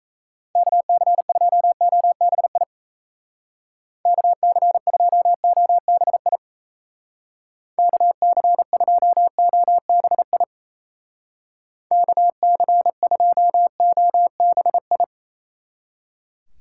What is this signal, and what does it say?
Morse code, heard as a keyed tone, about 34 wpm, 700 Hz. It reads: KC2OBI KC2OBI KC2OBI KC2OBI
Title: Speed Racing - USA 2x3 Call Signs 28wpm